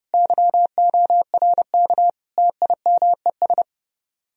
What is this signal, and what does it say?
Morse code, heard as a keyed tone, about 31 wpm, 700 Hz. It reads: YORK TIMES